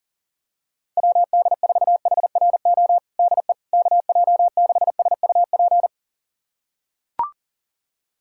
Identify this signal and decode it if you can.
Morse code, heard as a keyed tone, about 40 wpm, 700 Hz. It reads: WD4HRO DE KJ6SUP
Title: USA Callsign DE Callsign 40wpm